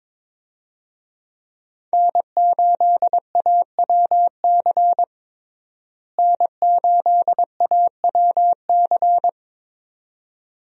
Morse code, heard as a keyed tone, about 22 wpm, 700 Hz. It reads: N8AWC N8AWC
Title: USA Call Signs - 5 Characters - Repeated 2x - 22wpm